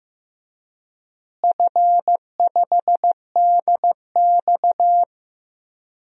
Morse code, heard as a keyed tone, about 15 wpm, 700 Hz. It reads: F5DX
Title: DXCC-100 Call Signs - 4 Characters - Repeated 3x - 15wpm